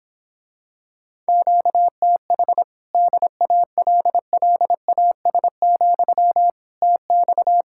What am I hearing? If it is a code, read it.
Morse code, heard as a keyed tone, about 26 wpm, 700 Hz. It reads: QTH DALLAS, TX
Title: Speed-Racing Common QSO Phrases - 22wpm